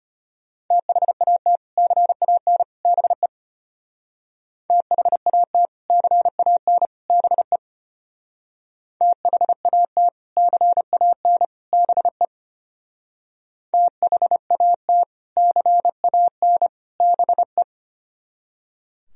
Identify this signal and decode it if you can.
Morse code, heard as a keyed tone, about 30 wpm, 700 Hz. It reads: THAT CAN BE THAT CAN BE THAT CAN BE THAT CAN BE
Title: Speed-Racing - Sets of 3 Words 25wpm